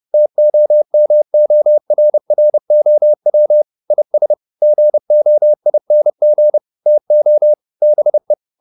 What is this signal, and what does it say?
Morse code, heard as a keyed tone, about 30 wpm, 600 Hz. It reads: TOMORROW IS GOING TO BE
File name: the old man and the sea / the old man and the sea 1 30wpm